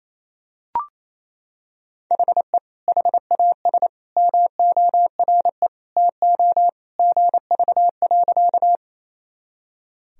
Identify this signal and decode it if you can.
Morse code, heard as a keyed tone, about 28 wpm, 700 Hz. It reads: HE HAS MORE TO GV.
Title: Sentences from Top 200 Words and Abbreviations 28wpm